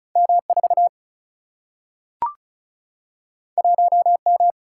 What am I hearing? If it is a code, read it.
Morse code, heard as a keyed tone, about 36 wpm, 700 Hz. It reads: M4 1M
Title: Single-Letter Single-Number 35wpm